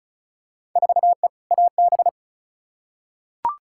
Morse code, heard as a keyed tone, about 34 wpm, 700 Hz. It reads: 4E AB
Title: Field Day Exchanges Encore - 35wpm